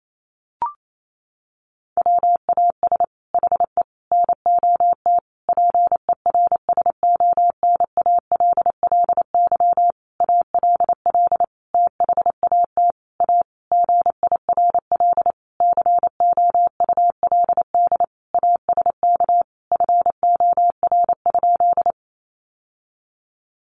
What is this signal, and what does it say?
Morse code, heard as a keyed tone, about 28 wpm, 700 Hz. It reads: WAS HE NOT PERSONALLY ALL THAT A GIRL COULD ASK FOR?
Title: Sentences from Top 1400 Words 28wpm